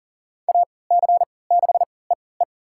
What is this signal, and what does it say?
Morse code, heard as a keyed tone, about 40 wpm, spaced as for 23 wpm, 700 Hz. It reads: AC6EE
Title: USA Call Signs - 5 Characters - 3x Character Spacing - 40wpm